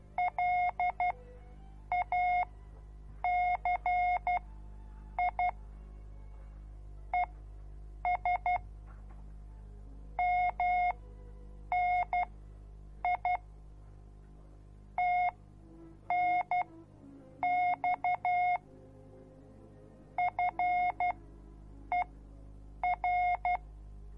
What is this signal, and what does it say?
Morse code, heard as a keyed tone, about 12 wpm, spaced as for 7 wpm, 720 Hz. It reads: LACI ES MNI TNX FER